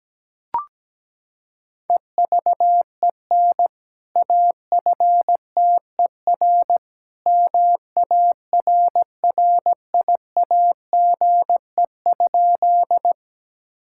Morse code, heard as a keyed tone, about 17 wpm, 700 Hz. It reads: EVEN AFTER MARRIAGE?
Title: Sentences from Top 1000 Words - Review Part 5 - 17wpm